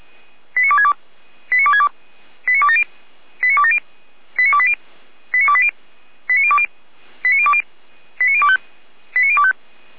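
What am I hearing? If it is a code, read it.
Five-tone selective call (ZVEI), heard as quick runs of five tones, tones of about 70 ms. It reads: 89262 89262 89270 89270 89280 89280 89020 89020 89025 89025
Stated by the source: ZVEI five-tone selective call received by SDR.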